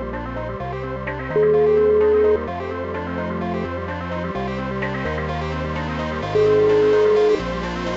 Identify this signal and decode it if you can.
Ringback tone (Europe, 425 Hz), with noise.